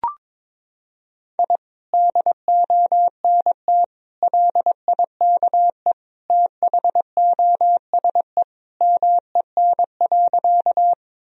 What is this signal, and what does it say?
Morse code, heard as a keyed tone, about 22 wpm, 700 Hz. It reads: I DONT LIKE THOSE MEN.
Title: Sentences from Top 1000 Words - Review Part 1 - 22wpm